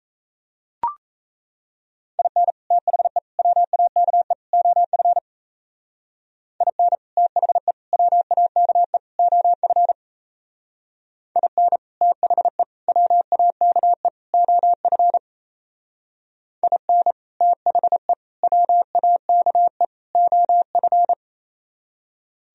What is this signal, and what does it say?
Morse code, heard as a keyed tone, about 33 wpm, 700 Hz. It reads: IN THE WAKE OF IN THE WAKE OF IN THE WAKE OF IN THE WAKE OF
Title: Speed-Racing - Sets of 4 Words 28wpm